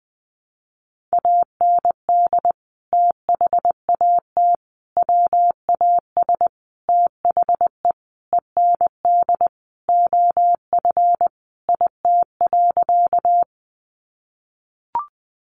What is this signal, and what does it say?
Morse code, heard as a keyed tone, about 20 wpm, 700 Hz. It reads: AND THAT WAS THE END OF IT.
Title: Sentences from Top 200 Words 20wpm